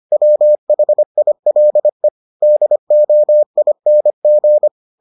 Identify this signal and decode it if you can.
Morse code, heard as a keyed tone, about 25 wpm, 600 Hz. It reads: WHILE DOING